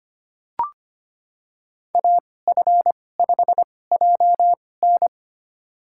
Morse code, heard as a keyed tone, about 25 wpm, spaced as for 18 wpm, 700 Hz. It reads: AF5JN
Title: USA Call Signs - 5 Characters - 2x Character Spacing - 25wpm